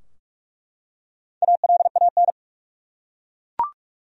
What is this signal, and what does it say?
Morse code, heard as a keyed tone, about 45 wpm, 700 Hz. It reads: ALAN
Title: USA Common Names - 45wpm